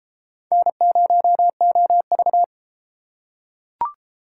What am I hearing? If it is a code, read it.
Morse code, heard as a keyed tone, about 28 wpm, 700 Hz. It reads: N0OV